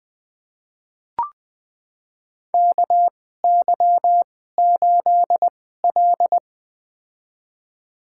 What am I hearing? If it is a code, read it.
Morse code, heard as a keyed tone, about 20 wpm, spaced as for 14 wpm, 700 Hz. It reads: KY8L
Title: USA Call Signs - 4 Characters - 2x Character Spacing - 20wpm